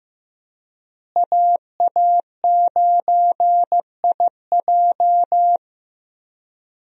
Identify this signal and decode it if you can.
Morse code, heard as a keyed tone, about 15 wpm, 700 Hz. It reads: AA9IJ
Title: USA Call Signs - 5 Characters - Repeated 2x - 15wpm